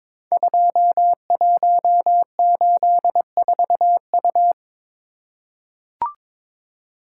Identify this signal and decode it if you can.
Morse code, heard as a keyed tone, about 22 wpm, 700 Hz. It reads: 2184U
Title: Code Groups - 5 Letter:Numbers - 22wpm